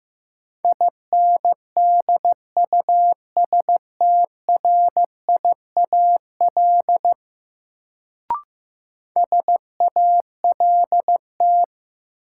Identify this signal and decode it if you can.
Morse code, heard as a keyed tone, about 15 wpm, 700 Hz. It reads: INDUSTRIAL SALT
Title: Sets of 5 Words - New Word Drill - 15wpm